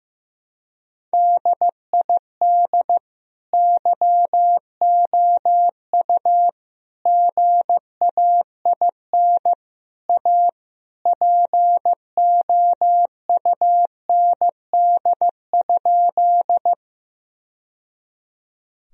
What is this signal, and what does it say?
Morse code, heard as a keyed tone, about 15 wpm, 700 Hz. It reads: DID YOU GAIN A POUND?
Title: Sentences from Top 1000 Words - Review Part 2 - 15wpm